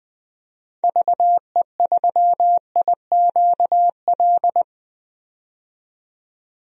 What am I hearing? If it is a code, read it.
Morse code, heard as a keyed tone, about 20 wpm, 700 Hz. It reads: VE3IQL
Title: DXCC-100 Call Signs - 6 Characters - 20wpm